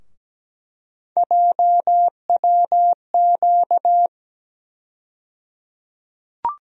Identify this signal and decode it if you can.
Morse code, heard as a keyed tone, about 17 wpm, 700 Hz. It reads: JWQ
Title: USA Call Sign Postfixes - 3 Letters 17wpm